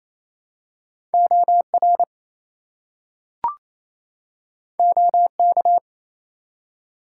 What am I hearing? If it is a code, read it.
Morse code, heard as a keyed tone, about 29 wpm, 700 Hz. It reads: OR OK
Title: Two Letter Words - 28wpm